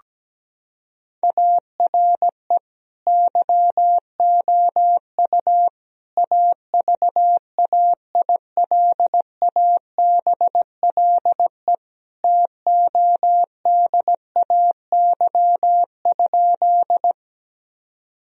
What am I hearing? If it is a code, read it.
Morse code, heard as a keyed tone, about 17 wpm, 700 Hz. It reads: ARE YOU AVAILABLE TODAY?
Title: Sentences from Top 1000 Words - Review Part 3 - 17wpm